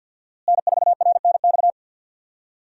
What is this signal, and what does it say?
Morse code, heard as a keyed tone, about 47 wpm, 700 Hz. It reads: N4RNX
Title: USA Call Signs - 5 Characters - Repeated 2x - 50wpm